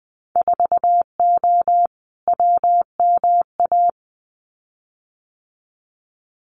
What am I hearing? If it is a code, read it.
Morse code, heard as a keyed tone, about 20 wpm, 700 Hz. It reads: 4O WMA
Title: Winter Field Day Exchanges Encore 20wpm